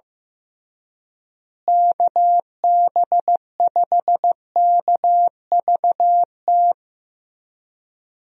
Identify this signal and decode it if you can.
Morse code, heard as a keyed tone, about 15 wpm, 700 Hz. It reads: KB5KVT
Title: USA Call Signs - 6 Characters - Repeated 3x - 15wpm